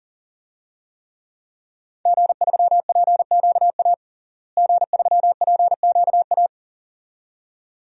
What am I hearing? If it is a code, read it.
Morse code, heard as a keyed tone, about 40 wpm, 700 Hz. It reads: G3PQA G3PQA
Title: DXCC-100 Call Signs - 5 Characters - Repeated 2x - 40wpm